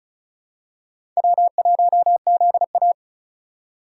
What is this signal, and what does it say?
Morse code, heard as a keyed tone, about 36 wpm, 700 Hz. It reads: W1ZA
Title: USA Call Signs - 4 Characters - 3x Character Spacing - 35wpm